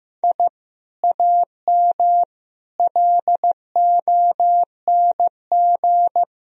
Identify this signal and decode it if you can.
Morse code, heard as a keyed tone, about 16 wpm, 700 Hz. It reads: I AM LONG
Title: Sentences from Top 1000 Words - Review Part 3 - 15wpm